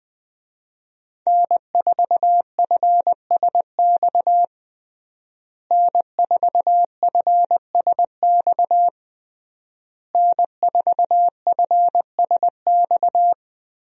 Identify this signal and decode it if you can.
Morse code, heard as a keyed tone, about 20 wpm, 700 Hz. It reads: N4FSX N4FSX N4FSX
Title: USA Call Signs - 5 Characters - Repeated 3x - 20wpm